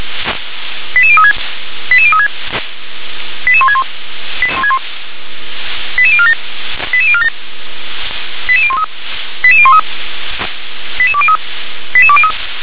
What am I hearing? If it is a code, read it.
Five-tone selective call (ZVEI), heard as quick runs of five tones, tones of about 70 ms. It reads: 88036 88036 88161 88047 88047 88013 88013 88203 88203